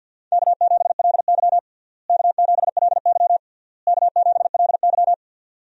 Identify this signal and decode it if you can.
Morse code, heard as a keyed tone, about 52 wpm, 700 Hz. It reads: K7LY K7LY K7LY
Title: USA Call Signs - 4 Characters - Repeated 3x - 50wpm